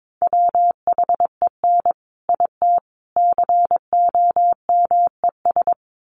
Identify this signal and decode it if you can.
Morse code, heard as a keyed tone, about 22 wpm, 700 Hz. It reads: WHEN IT COMES
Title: Sets of 3 Words 22wpm